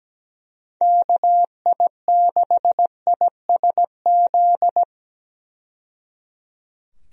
Morse code, heard as a keyed tone, about 17 wpm, 700 Hz. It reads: KI6ISZ
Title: USA Call Signs - 6 Characters - 17wpm